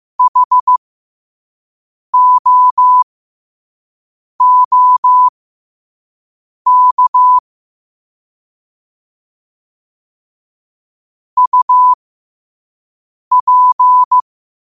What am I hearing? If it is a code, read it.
Morse code, heard as a keyed tone, about 15 wpm, spaced as for 5 wpm, 1000 Hz. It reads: HOOK UP
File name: qotdfiles / 5wpm